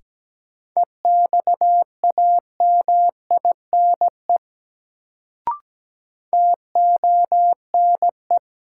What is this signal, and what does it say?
Morse code, heard as a keyed tone, about 17 wpm, 700 Hz. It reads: EXAMINE TONE
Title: Sentences from Top 1600 Words - New Word Drill - 17wpm